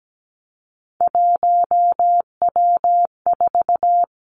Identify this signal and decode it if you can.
Morse code, heard as a keyed tone, about 17 wpm, 700 Hz. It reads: 1W4
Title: Three Challenging Letter-Number Combinations 17wpm